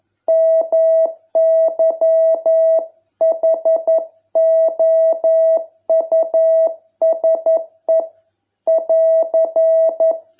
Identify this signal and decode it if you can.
Morse code, heard as a keyed tone, about 11 wpm, 660 Hz. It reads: MY HOUSE <AR>